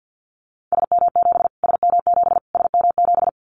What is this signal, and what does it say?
Morse code, heard as a keyed tone, about 50 wpm, 700 Hz. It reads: SM7 SM7 SM7
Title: Callsign Prefixes for Top 25 DXCC - Repeated 3x - 50wpm